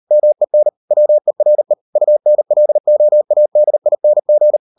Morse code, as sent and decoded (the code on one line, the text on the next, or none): -- . -. / .-- . .-. / ..- -. .-.. --- .- -.. .. -. --.
MEN WERE UNLOADING